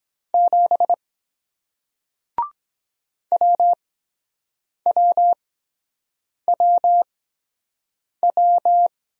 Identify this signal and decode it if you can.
Morse code, heard as a keyed tone, about 23 wpm, spaced as for 7 wpm, 700 Hz. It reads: 7 WWWW